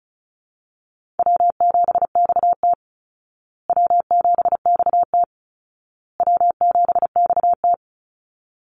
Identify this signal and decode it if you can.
Morse code, heard as a keyed tone, about 34 wpm, 700 Hz. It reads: W7XT W7XT W7XT
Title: Call Signs Repeated 2x - 35wpm